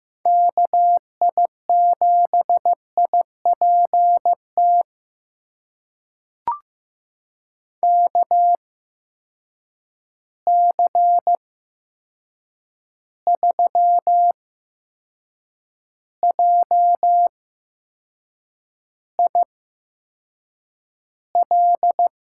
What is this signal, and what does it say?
Morse code, heard as a keyed tone, about 15 wpm, spaced as for 6 wpm, 700 Hz. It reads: KI7IPT KC3JIL
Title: USA Call Signs - 6 Characters - 8x Character Spacing - 15wpm